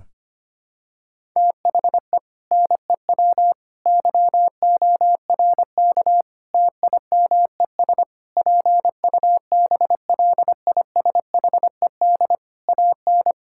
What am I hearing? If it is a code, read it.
Morse code, heard as a keyed tone, about 25 wpm, 700 Hz. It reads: THE NEW YORK TIMES PUBLISHED AN
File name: practice / 25wpm / Sentences with 4-Word Phrases - 25wpm